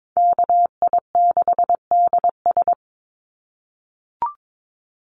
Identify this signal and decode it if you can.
Morse code, heard as a keyed tone, about 22 wpm, 700 Hz. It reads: KI6DS
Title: NA SOTA Chaser Callsigns 22wpm